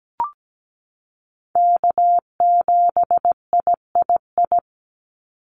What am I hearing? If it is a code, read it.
Morse code, heard as a keyed tone, about 17 wpm, 700 Hz. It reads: K7III
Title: NA SOTA Chaser Callsigns 17wpm